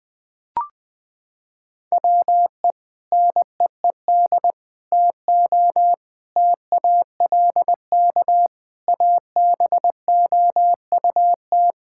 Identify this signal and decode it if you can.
Morse code, heard as a keyed tone, about 19 wpm, 700 Hz. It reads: WE NEED TO TALK ABOUT
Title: Sentences with 3-Word Phrases - 20wpm